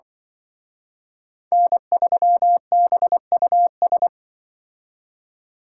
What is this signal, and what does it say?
Morse code, heard as a keyed tone, about 24 wpm, 700 Hz. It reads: N3BUS
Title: Speed Racing - USA Call Signs 20wpm